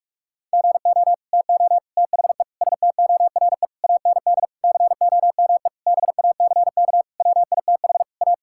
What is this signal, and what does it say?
Morse code, heard as a keyed tone, about 46 wpm, 700 Hz. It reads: GO TO THE STORE AND COME BACK WITH A
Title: Sentences from Top 1000 Words - Review Part 5 - 45wpm